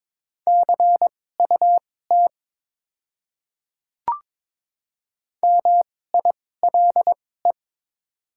Morse code, heard as a keyed tone, about 22 wpm, spaced as for 16 wpm, 700 Hz. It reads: CUT MILE
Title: 2x Character Spacing - Top 300 Words - 22wpm